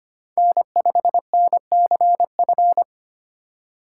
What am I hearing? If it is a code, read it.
Morse code, heard as a keyed tone, about 25 wpm, 700 Hz. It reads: N5NCF
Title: USA Call Signs - 5 Characters - 6x Character Spacing - 25wpm